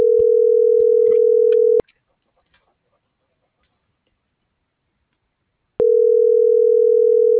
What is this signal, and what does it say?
Ringback tone (North America), 440 + 480 Hz.